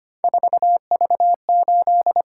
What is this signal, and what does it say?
Morse code, heard as a keyed tone, about 25 wpm, 700 Hz. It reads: 4V8
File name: Three Letter-Number Combinations - 25wpm